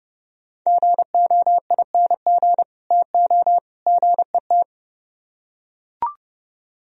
Morse code, heard as a keyed tone, about 30 wpm, 700 Hz. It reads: GOING TO GET